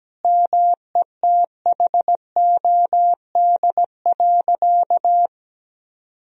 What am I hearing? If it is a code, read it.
Morse code, heard as a keyed tone, about 17 wpm, 700 Hz. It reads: METHOD.
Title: Sentences from Top 1000 Words - Review Part 3 - 17wpm